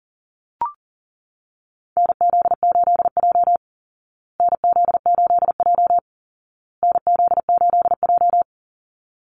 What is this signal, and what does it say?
Morse code, heard as a keyed tone, about 40 wpm, 700 Hz. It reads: NZ8J NZ8J NZ8J
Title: Call Signs Repeated 2x - 40wpm